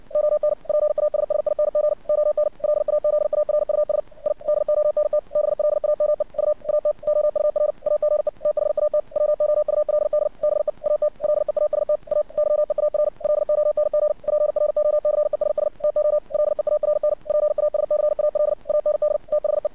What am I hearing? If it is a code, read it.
Morse code, heard as a keyed tone, about 60 wpm, 600 Hz. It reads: ON PASSING ON LAZARUS I DONT BLAME U AT OUR AGE THAT WOULD BE AT LEAST A YEAR LONG PROCESS TO LEARN PASCAL AND THE
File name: archived / CRSnet